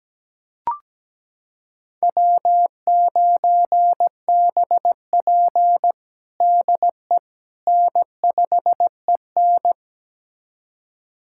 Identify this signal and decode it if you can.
Morse code, heard as a keyed tone, about 17 wpm, 700 Hz. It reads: W9BP DE N5EN
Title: USA Callsign DE Callsign - 4 Characters - 17wpm